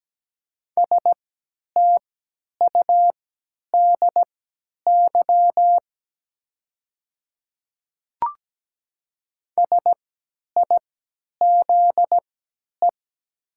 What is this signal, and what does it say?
Morse code, heard as a keyed tone, about 17 wpm, spaced as for 10 wpm, 700 Hz. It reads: STUDY SIZE
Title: 3x Character Spacing - Top 500 Words - 17wpm